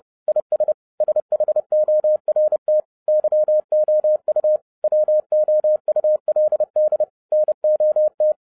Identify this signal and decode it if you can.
Morse code, heard as a keyed tone, about 30 wpm, 610 Hz. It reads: IS SHORT YOU WOULD NOT
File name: 2012-01-mp3 / n11-I DIDNT MISS THE SOLAR CYCLE 24-30wpm